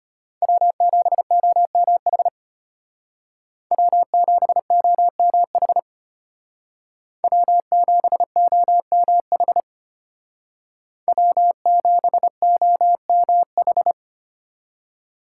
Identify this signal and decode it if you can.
Morse code, heard as a keyed tone, about 31 wpm, 700 Hz. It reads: W7OMH W7OMH W7OMH W7OMH